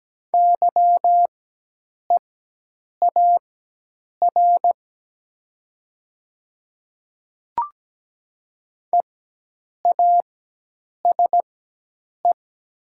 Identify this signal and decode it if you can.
Morse code, heard as a keyed tone, about 17 wpm, spaced as for 8 wpm, 700 Hz. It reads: YEAR EASE
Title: 4x Character Spacing - Top 300 Words - 17wpm